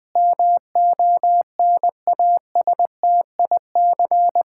Morse code, heard as a keyed tone, about 20 wpm, 700 Hz. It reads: MONASTIC
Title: Pangram Challenge v2 - 20wpm